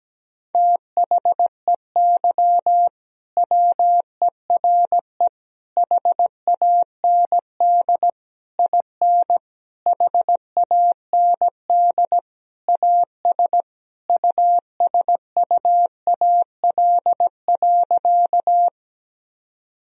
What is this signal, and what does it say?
Morse code, heard as a keyed tone, about 17 wpm, 700 Hz. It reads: THEY WERE HAND IN HAND AS USUAL.
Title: Sentences from Top 1000 Words - Review Part 2 - 17wpm